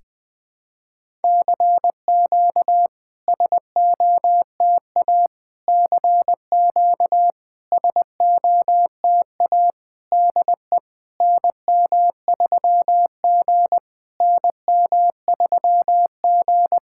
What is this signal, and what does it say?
Morse code, heard as a keyed tone, about 20 wpm, 700 Hz. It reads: CQ SOTA CQ SOTA DE NM3G NM3G
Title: SOTA QSO Practice 20wpm